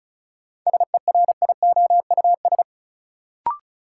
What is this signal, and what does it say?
Morse code, heard as a keyed tone, about 36 wpm, 700 Hz. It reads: SERIOUS
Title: Sentences from Top 1100 Words - New Word Drill - 35wpm